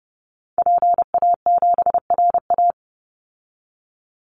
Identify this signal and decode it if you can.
Morse code, heard as a keyed tone, about 30 wpm, 700 Hz. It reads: PA7RA
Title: Call Signs Repeated 3x - 30wpm